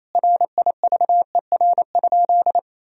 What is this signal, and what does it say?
Morse code, heard as a keyed tone, about 28 wpm, 700 Hz. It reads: RIVER?
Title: Sentences from Top 1000 Words - Review Part 1 - 28wpm